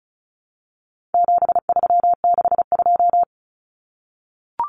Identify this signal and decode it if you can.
Morse code, heard as a keyed tone, about 34 wpm, 700 Hz. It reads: 7362
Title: Numbers - 4-Digit 35wpm